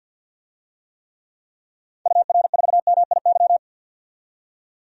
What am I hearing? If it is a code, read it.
Morse code, heard as a keyed tone, about 53 wpm, 700 Hz. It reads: UR4MIY
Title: DXCC-100 Call Signs - 6 Characters - 50wpm